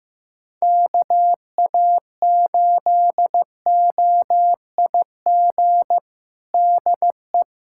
Morse code, heard as a keyed tone, about 15 wpm, 700 Hz. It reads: KA8OIG DE
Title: USA Callsign DE Callsign - 6 Characters - 15wpm